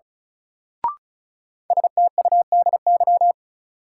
Morse code, heard as a keyed tone, about 36 wpm, 700 Hz. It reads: STUDY
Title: Sentences from Top 300 Words - New Word Drill - 35wpm